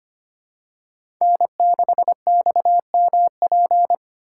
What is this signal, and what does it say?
Morse code, heard as a keyed tone, about 25 wpm, 700 Hz. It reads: N6XMP
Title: USA Call Signs - 5 Characters - 3x Character Spacing - 25wpm